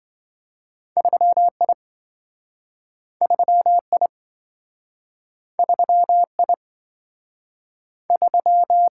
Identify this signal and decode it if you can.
Morse code, heard as a keyed tone, about 28 wpm, 700 Hz. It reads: 3I 3I 3I 3